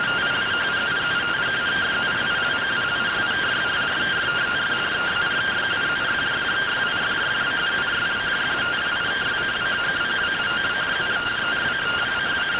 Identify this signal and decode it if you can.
Two-tone teleprinter signal, about 170 Hz shift.